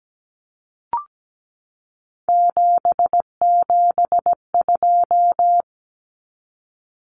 Morse code, heard as a keyed tone, about 17 wpm, 700 Hz. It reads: 772
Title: Numbers - 3-Digit 17wpm